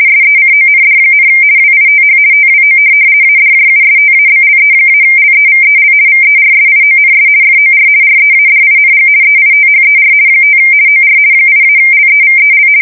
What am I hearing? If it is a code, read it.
RTTY, about 50 baud. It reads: SECOND, WITH TWO ADJUSTABLE-LEVEL VERSIONS FOR READING EITHER 5 OR 8-LEVEL
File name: fsk170-50B / teletype-corp